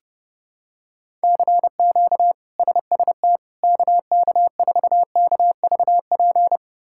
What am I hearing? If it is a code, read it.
Morse code, heard as a keyed tone, about 30 wpm, 700 Hz. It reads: CQ SST KK4KVP
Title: K1USN SST QSO Practice - 30wpm